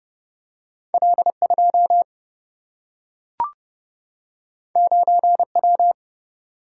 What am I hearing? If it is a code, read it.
Morse code, heard as a keyed tone, about 30 wpm, 700 Hz. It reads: L2 9W
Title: Single-Letter Single-Number 30wpm